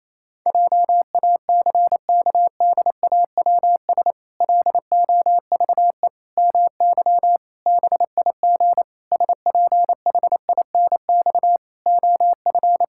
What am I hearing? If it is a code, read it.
Morse code, heard as a keyed tone, about 28 wpm, 700 Hz. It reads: JACKDAWS LOVE MY BIG SPHINX OF
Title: Pangram Challenge v2 - 28wpm